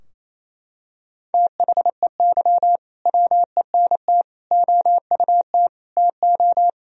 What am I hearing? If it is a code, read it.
Morse code, heard as a keyed tone, about 28 wpm, 700 Hz. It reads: THEY WENT OUT TO
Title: Sentences from Top 1000 Words - Review Part 2 - 28wpm